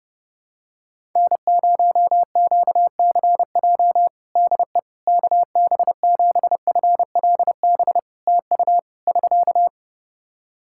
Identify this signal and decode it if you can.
Morse code, heard as a keyed tone, about 30 wpm, 700 Hz. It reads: N0QCJ DE KB7FLB TU <SK>
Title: Practice QSO - Exchange 4 - Ending - 30wpm